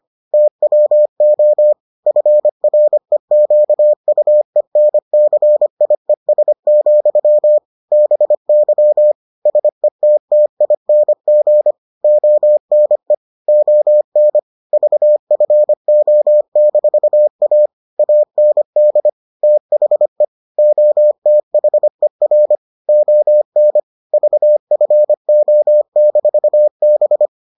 Morse code, heard as a keyed tone, about 25 wpm, 600 Hz. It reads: TWO FREQUENCIES, BY SETTING ONE ON VFO-A AND THE OTHER ON VFO-B